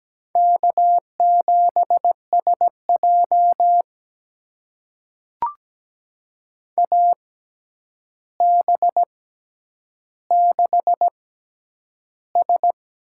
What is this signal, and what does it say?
Morse code, heard as a keyed tone, about 17 wpm, spaced as for 9 wpm, 700 Hz. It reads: K7SJ AB6S
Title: USA Call Signs - 4 Characters - 6x Character Spacing - 17wpm